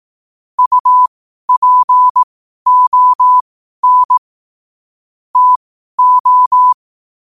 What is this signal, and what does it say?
Morse code, heard as a keyed tone, about 18 wpm, spaced as for 12 wpm, 1000 Hz. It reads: UPON TO
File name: qotdfiles / 12wpm